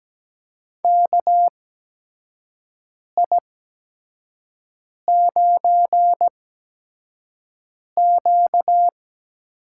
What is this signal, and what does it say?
Morse code, heard as a keyed tone, about 17 wpm, spaced as for 5 wpm, 700 Hz. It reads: KI9Q
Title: USA Call Signs - 4 Characters - 8x Character Spacing - 17wpm